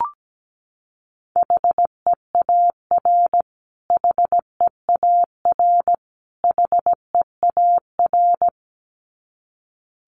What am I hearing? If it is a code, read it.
Morse code, heard as a keyed tone, about 17 wpm, 700 Hz. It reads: HEAR HEAR HEAR
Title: Top 300 Words Repeated 17wpm